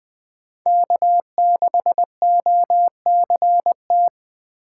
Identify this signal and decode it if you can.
Morse code, heard as a keyed tone, about 24 wpm, 700 Hz. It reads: K6OCT
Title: Speed Racing - USA 1x3 Call Signs 20wpm